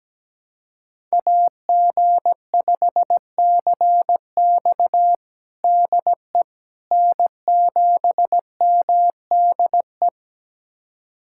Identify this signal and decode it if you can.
Morse code, heard as a keyed tone, about 17 wpm, 700 Hz. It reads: AG5CX DE N7MDE
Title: USA Callsign DE Callsign - 5 Characters - 17wpm